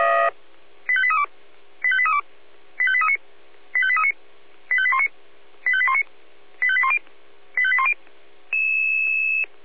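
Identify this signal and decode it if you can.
Five-tone selective call (ZVEI), heard as quick runs of five tones, tones of about 70 ms. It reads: 85832 85832 85839 85839 85819 85819 85810 85810
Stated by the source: ZVEI five-tone selective call 85830 received with an SDR.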